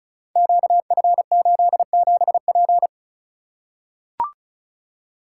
Morse code, heard as a keyed tone, about 36 wpm, 690 Hz. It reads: QF87P
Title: Code Groups - 5 Letter:Numbers - 35wpm